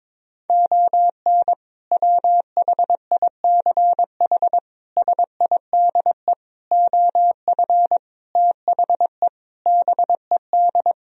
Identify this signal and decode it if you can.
Morse code, heard as a keyed tone, about 22 wpm, 700 Hz. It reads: ON WHICH SIDE OF THE BED
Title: Sentences from Top 1000 Words - Review Part 1 - 22wpm